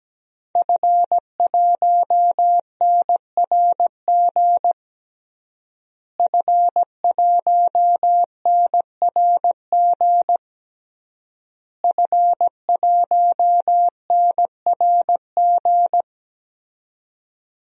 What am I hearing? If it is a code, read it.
Morse code, heard as a keyed tone, about 17 wpm, 700 Hz. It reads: F1NRG F1NRG F1NRG
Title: DXCC-100 Call Signs - 5 Characters - Repeated 3x - 17wpm